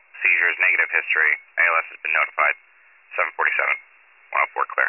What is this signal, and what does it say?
Emergency Alert System (SAME) tones, heard as harsh buzzing bursts, with speech. It is heard from 1.1 to 3.8 s.